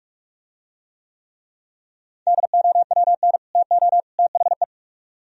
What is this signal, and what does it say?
Morse code, heard as a keyed tone, about 45 wpm, 700 Hz. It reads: DOWN TO THE